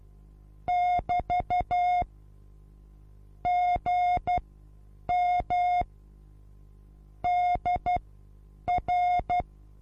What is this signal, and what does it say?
Morse code, heard as a keyed tone, about 12 wpm, spaced as for 8 wpm, 720 Hz. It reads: <BT> GM DR
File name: Onlinelehrgang / morsen / DJ4UF Morsen23